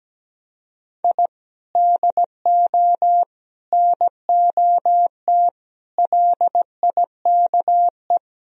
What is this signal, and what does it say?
Morse code, heard as a keyed tone, about 17 wpm, 700 Hz. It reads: I DO NOT LIKE
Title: Speed-Racing - Sets of 4 Words 17wpm